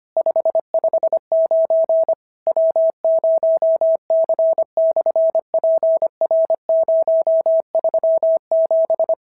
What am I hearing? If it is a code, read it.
Morse code, heard as a keyed tone, about 25 wpm, 650 Hz. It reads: 559 W0C/PR037
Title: SOTA QSO Practice 25wpm